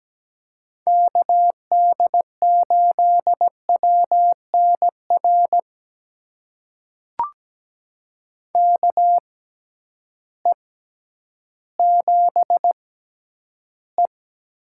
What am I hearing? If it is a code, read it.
Morse code, heard as a keyed tone, about 17 wpm, spaced as for 10 wpm, 700 Hz. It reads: KD8WNR KE7E
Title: USA Call Signs - 6 Characters - 6x Character Spacing - 17wpm